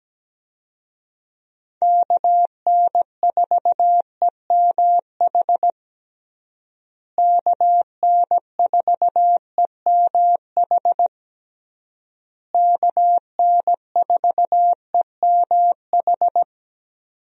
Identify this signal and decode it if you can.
Morse code, heard as a keyed tone, about 17 wpm, 700 Hz. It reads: KN4EMH KN4EMH KN4EMH
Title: USA Call Signs - 6 Characters - Repeated 3x - 17wpm